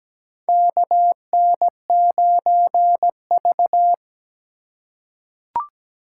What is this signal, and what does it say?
Morse code, heard as a keyed tone, about 17 wpm, 700 Hz. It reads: KN9V
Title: USA Call Signs - 4 Characters - 3x Character Spacing - 17wpm